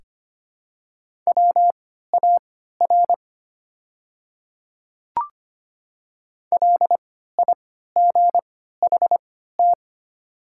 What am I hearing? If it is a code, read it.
Morse code, heard as a keyed tone, about 25 wpm, spaced as for 14 wpm, 700 Hz. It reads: WAR LIGHT
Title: 3x Character Spacing - Top 500 Words - 25wpm